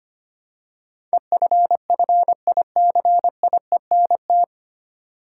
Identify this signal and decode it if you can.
Morse code, heard as a keyed tone, about 25 wpm, 700 Hz. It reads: EFFICIENT
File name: Nine Letter Words - 25wpm